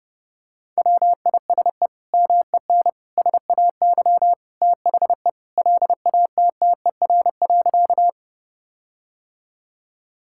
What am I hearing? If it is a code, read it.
Morse code, heard as a keyed tone, about 31 wpm, 700 Hz. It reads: WISE MEN SAY THE LATTER.
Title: Sentences from Top 1600 Words 30wpm